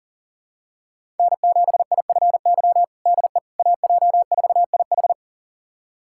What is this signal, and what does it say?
Morse code, heard as a keyed tone, about 40 wpm, 700 Hz. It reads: N7IFY DE AJ4IH
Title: USA Callsign DE Callsign - 5 Characters - 40wpm